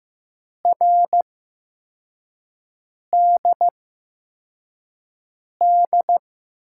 Morse code, heard as a keyed tone, about 15 wpm, spaced as for 4 wpm, 700 Hz. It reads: RDD
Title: USA Call Signs - 5 Characters - 8x Character Spacing - 15wpm